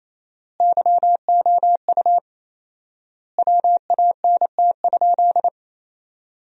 Morse code, heard as a keyed tone, about 28 wpm, 700 Hz. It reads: YOU WANT?
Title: 4x Word-Spacing - Sentences from Top 400 Words - 28wpm